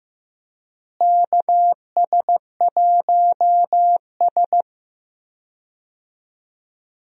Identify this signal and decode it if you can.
Morse code, heard as a keyed tone, about 15 wpm, 700 Hz. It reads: KS1S
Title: USA Call Signs - 4 Characters - Repeated 2x - 15wpm